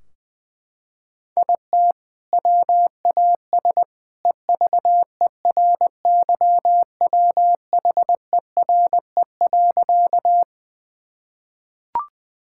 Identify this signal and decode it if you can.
Morse code, heard as a keyed tone, about 20 wpm, 700 Hz. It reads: IT WAS EVERYWHERE.
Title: Sentences from Top 1600 Words 20wpm